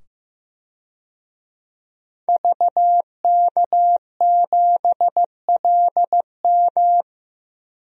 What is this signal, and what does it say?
Morse code, heard as a keyed tone, about 15 wpm, 700 Hz. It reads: VK7LM
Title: DXCC-100 Call Signs - 5 Characters - 15wpm